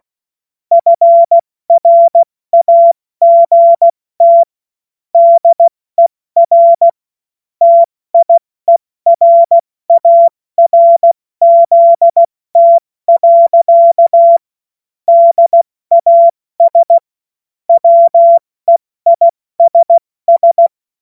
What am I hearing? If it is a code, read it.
Morse code, heard as a keyed tone, about 16 wpm, spaced as for 14 wpm, 680 Hz. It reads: FRAGT DER TIERARZT. DAS WEISS
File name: Morsekurs / Witze / Witz 10 Juni Mai 2025 70 F80